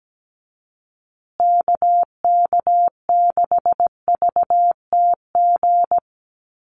Morse code, heard as a keyed tone, about 17 wpm, 700 Hz. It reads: KK6VTG
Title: USA Call Signs Repeated 2x 17wpm